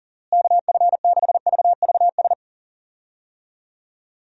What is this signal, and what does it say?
Morse code, heard as a keyed tone, about 37 wpm, 700 Hz. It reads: KF6VVS
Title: USA Call Signs - 6 Characters - Repeated 3x - 40wpm